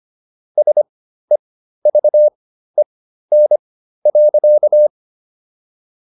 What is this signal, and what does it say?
Morse code, heard as a keyed tone, about 25 wpm, spaced as for 13 wpm, 600 Hz. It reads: SEVEN.